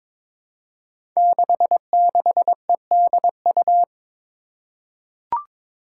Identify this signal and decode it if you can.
Morse code, heard as a keyed tone, about 22 wpm, 700 Hz. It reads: 66EDU
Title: Five Challenging Letter-Number Combinations 22wpm